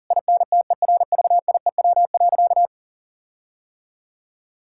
Morse code, heard as a keyed tone, about 40 wpm, 700 Hz. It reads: INTERVIEW.
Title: Sentences from Top 1600 Words 40wpm